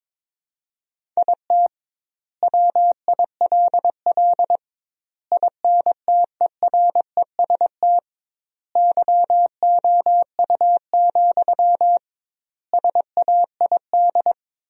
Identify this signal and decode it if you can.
Morse code, heard as a keyed tone, about 22 wpm, 700 Hz. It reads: IT WILL INTEREST YOU, SAID
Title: 2x Word-Spacing - Sentences from Top 400 Words - 22wpm